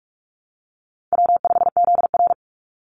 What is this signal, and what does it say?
Morse code, heard as a keyed tone, about 44 wpm, 700 Hz. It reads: W5ZR
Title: Easy Call Signs Repeated 3x - 45wpm